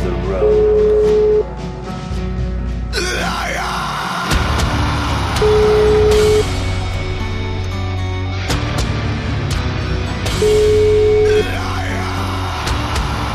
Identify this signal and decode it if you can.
Ringback tone (Europe, 425 Hz), with noise.